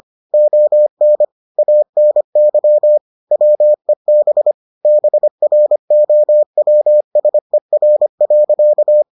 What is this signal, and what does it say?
Morse code, heard as a keyed tone, about 25 wpm, 600 Hz. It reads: ON ANY WEB BROWSER.